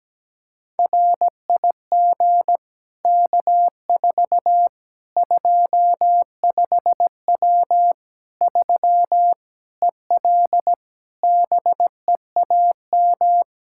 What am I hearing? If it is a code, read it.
Morse code, heard as a keyed tone, about 17 wpm, 700 Hz. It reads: RIG K4 25W 3 EL BEAM
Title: Practice QSO - Exchange 2 - Info - 17wpm